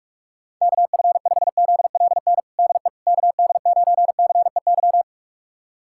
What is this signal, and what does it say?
Morse code, heard as a keyed tone, about 46 wpm, 700 Hz. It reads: KF5ZLN DE KD9CEY
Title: USA Callsign DE Callsign 45wpm